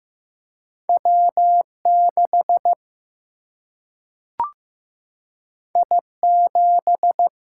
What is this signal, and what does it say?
Morse code, heard as a keyed tone, about 15 wpm, 700 Hz. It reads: W 6 I 7
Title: Single-Letter Single-Number 15wpm